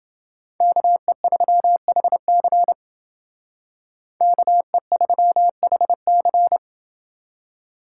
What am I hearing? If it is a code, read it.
Morse code, heard as a keyed tone, about 28 wpm, 700 Hz. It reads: KE3HC KE3HC